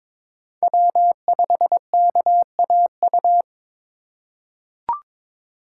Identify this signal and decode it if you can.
Morse code, heard as a keyed tone, about 22 wpm, 700 Hz. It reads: W5KAU
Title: USA Call Signs - 6x Character Spacing - 22wpm